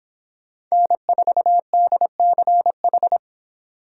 Morse code, heard as a keyed tone, about 26 wpm, 700 Hz. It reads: N4DCH